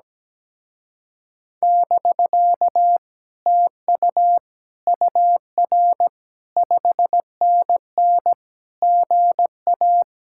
Morse code, heard as a keyed tone, about 17 wpm, 700 Hz. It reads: <BK> TU UR 5NN GA
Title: Speed-Racing Common QSO Phrases - 17wpm